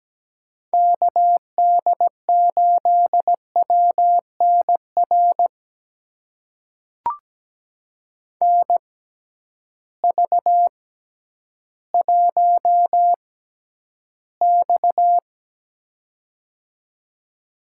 Morse code, heard as a keyed tone, about 17 wpm, spaced as for 10 wpm, 700 Hz. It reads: KD8WNR NV1X
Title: USA Call Signs - 6x Character Spacing - 17wpm